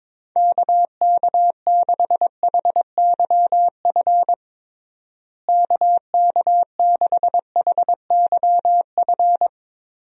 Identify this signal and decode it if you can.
Morse code, heard as a keyed tone, about 22 wpm, 700 Hz. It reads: KK6HYF KK6HYF
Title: USA Call Signs - 6 Characters - Repeated 3x - 22wpm